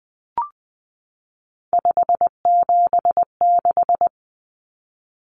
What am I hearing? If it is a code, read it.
Morse code, heard as a keyed tone, about 20 wpm, 700 Hz. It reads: 576
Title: Common Cut Numbers - 3-Digit 20wpm